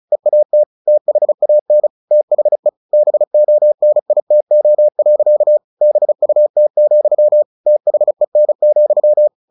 Morse code, heard as a keyed tone, about 34 wpm, 600 Hz. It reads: EAT THAN THE BONITO. BUT, THEN,
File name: the old man and the sea 2 35wpm